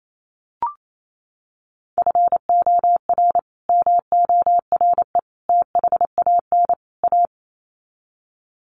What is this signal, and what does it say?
Morse code, heard as a keyed tone, about 28 wpm, 700 Hz. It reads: FOR MORE THAN A
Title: Sets of 4 Words 28wpm